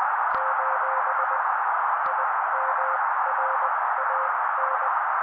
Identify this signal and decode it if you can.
Morse code, heard as a keyed tone, about 20 wpm, spaced as for 16 wpm, 530 Hz. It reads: : IMRAN
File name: daily / 16wpm noise